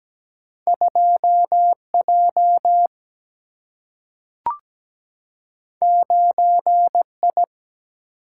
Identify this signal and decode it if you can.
Morse code, heard as a keyed tone, about 17 wpm, 700 Hz. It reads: 2J 9I
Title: Single-Letter Single-Number 17wpm